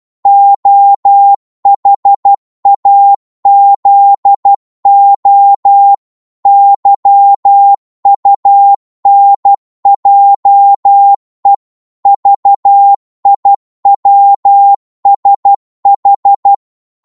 Morse code, heard as a keyed tone, about 12 wpm, 800 Hz. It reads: OHAZO YUNJE VIWSH
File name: Morse / ltr-12wpm